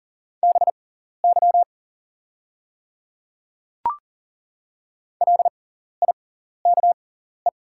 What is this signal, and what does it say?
Morse code, heard as a keyed tone, about 40 wpm, spaced as for 14 wpm, 700 Hz. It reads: BY LIKE
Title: 6x Character Spacing - Top 100 Words - 40wpm